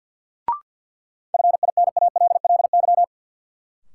Morse code, heard as a keyed tone, about 53 wpm, 700 Hz. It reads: FINALLY
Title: Sentences from Top 1000 Words - New Word Drill - 50wpm